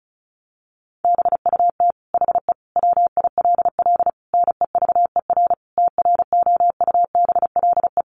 Morse code, heard as a keyed tone, about 34 wpm, 700 Hz. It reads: BUT HE WILL NEVER TROUBLE
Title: Sentences from Top 1200 Words 35wpm